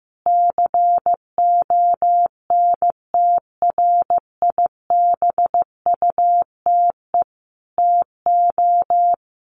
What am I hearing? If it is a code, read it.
Morse code, heard as a keyed tone, about 15 wpm, 700 Hz. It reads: CONTRIBUTE TO
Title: Sentences from Top 1400 Words 15wpm